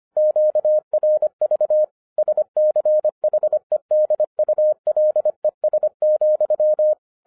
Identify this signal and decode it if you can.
Morse code, heard as a keyed tone, about 25 wpm, 610 Hz. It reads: QRV SCHEDULES,